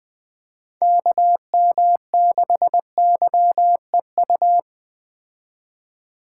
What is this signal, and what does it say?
Morse code, heard as a keyed tone, about 20 wpm, 700 Hz. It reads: KM6YEU
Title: Speed Racing - USA Call Signs 17wpm